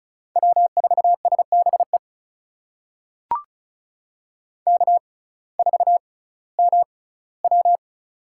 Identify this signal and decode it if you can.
Morse code, heard as a keyed tone, about 36 wpm, spaced as for 19 wpm, 700 Hz. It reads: W4SBE K4MW
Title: USA Call Signs - 6x Character Spacing - 35wpm